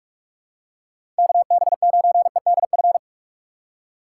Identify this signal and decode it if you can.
Morse code, heard as a keyed tone, about 42 wpm, 700 Hz. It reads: KB9EDF
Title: USA Call Signs - 6 Characters - 4x Character Spacing - 45wpm